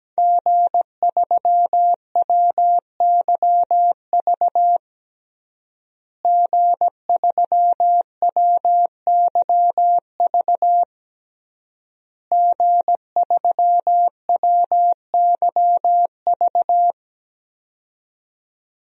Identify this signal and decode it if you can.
Morse code, heard as a keyed tone, about 17 wpm, 700 Hz. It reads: G3WYV G3WYV G3WYV
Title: DXCC-100 Call Signs - 5 Characters - Repeated 3x - 17wpm